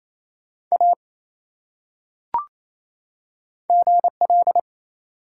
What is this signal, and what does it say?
Morse code, heard as a keyed tone, about 28 wpm, 700 Hz. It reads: A GL
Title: Top 180 QSO Elements 28wpm